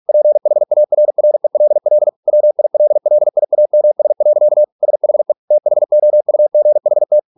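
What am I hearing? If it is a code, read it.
Morse code, heard as a keyed tone, about 48 wpm, 600 Hz. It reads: PHARRELL WILLIAMS. SHE THOUGHT
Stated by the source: Morse practice at 45 wpm, the novel Never by Ken Follett in English, from CW med Gnister.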